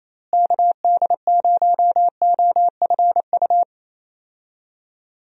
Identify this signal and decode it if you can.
Morse code, heard as a keyed tone, about 27 wpm, 700 Hz. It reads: KD0OFU
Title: USA Call Signs - 6 Characters - Repeated 2x - 28wpm